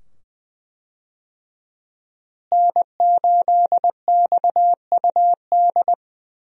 Morse code, heard as a keyed tone, about 20 wpm, 700 Hz. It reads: N8XUD
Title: USA Call Signs - 5 Characters - Repeated 2x - 20wpm